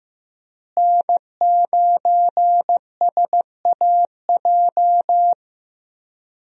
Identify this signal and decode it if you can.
Morse code, heard as a keyed tone, about 14 wpm, 700 Hz. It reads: N9SAJ
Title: USA Call Signs - 5 Characters - Repeated 3x - 15wpm